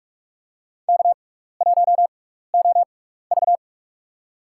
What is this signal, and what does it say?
Morse code, heard as a keyed tone, about 46 wpm, spaced as for 16 wpm, 700 Hz. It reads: K1OV
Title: USA Call Signs - 4 Characters - 6x Character Spacing - 45wpm